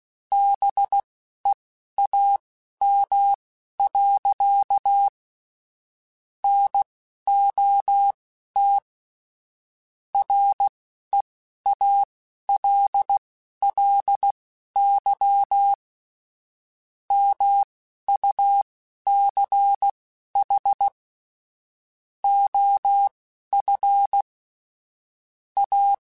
Morse code, heard as a keyed tone, about 16 wpm, spaced as for 11 wpm, 800 Hz. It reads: BEAM. NOT REALLY MUCH OF A
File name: QSOs / QSOs-mp3s / WD0BGP 1@13